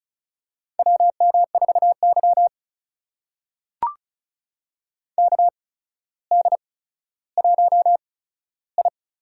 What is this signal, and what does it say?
Morse code, heard as a keyed tone, about 36 wpm, spaced as for 15 wpm, 700 Hz. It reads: WM4Y KD1I
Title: USA Call Signs - 4 Characters - 8x Character Spacing - 35wpm